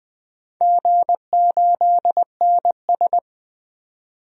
Morse code, heard as a keyed tone, about 20 wpm, 700 Hz. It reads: G8NS
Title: DXCC-100 Call Signs - 4 Characters - 20wpm